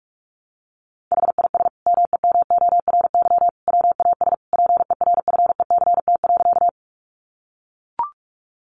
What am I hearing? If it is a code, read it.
Morse code, heard as a keyed tone, about 44 wpm, 700 Hz. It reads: HIS MEMORY WAS PERFECT.
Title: Sentences from Top 1300 Words 45wpm